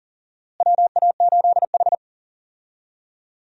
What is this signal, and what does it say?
Morse code, heard as a keyed tone, about 40 wpm, 700 Hz. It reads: WA8H
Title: USA Call Signs - 4 Characters - Repeated 3x - 40wpm